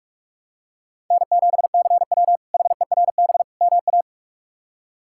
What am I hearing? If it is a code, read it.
Morse code, heard as a keyed tone, about 44 wpm, 700 Hz. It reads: N7CW HERB MA
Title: K1USN SST QSO Practice - 45wpm